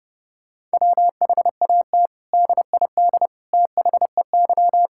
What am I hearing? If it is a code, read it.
Morse code, heard as a keyed tone, about 30 wpm, 700 Hz. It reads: WHAT DID THEY
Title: Sentences from Top 1000 Words - Review Part 5 - 30wpm